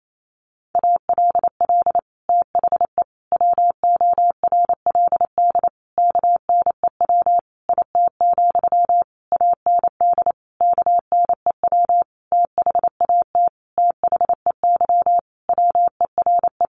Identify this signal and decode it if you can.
Morse code, heard as a keyed tone, about 28 wpm, 700 Hz. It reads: ALL THE WORLD KNEW IT, AND KNEW THAT THEY WERE
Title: Sentences from Top 1200 Words 28wpm